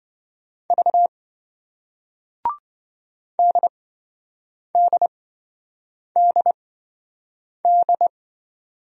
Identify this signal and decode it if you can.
Morse code, heard as a keyed tone, about 26 wpm, spaced as for 7 wpm, 700 Hz. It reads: V DDDD